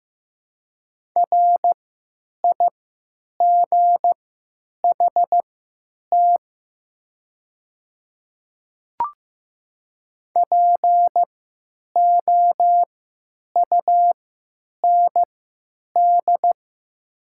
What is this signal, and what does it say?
Morse code, heard as a keyed tone, about 15 wpm, spaced as for 9 wpm, 700 Hz. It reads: RIGHT POUND
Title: 3x Character Spacing - Top 500 Words - 15wpm